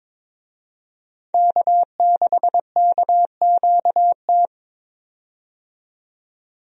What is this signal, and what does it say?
Morse code, heard as a keyed tone, about 22 wpm, 700 Hz. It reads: K6KQT
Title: USA Call Signs - 5 Characters - 22wpm